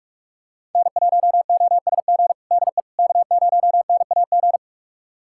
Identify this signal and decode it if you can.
Morse code, heard as a keyed tone, about 46 wpm, 700 Hz. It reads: N1OSG DE K0NAG
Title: USA Callsign DE Callsign - 5 Characters - 45wpm